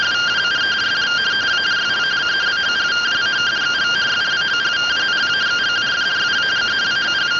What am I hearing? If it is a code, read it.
Two-tone teleprinter signal, about 170 Hz shift.